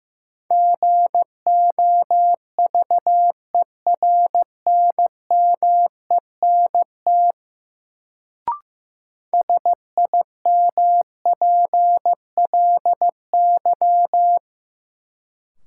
Morse code, heard as a keyed tone, about 15 wpm, 700 Hz. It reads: GOVERNMENT SIMPLY
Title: Sentences from Top 600 Words - New Word Drill - 15wpm